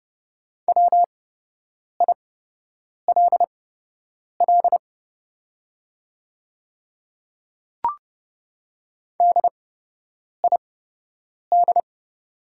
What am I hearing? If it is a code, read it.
Morse code, heard as a keyed tone, about 30 wpm, spaced as for 8 wpm, 700 Hz. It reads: WILL DID
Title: 8x Character Spacing - Top 100 Words - 30wpm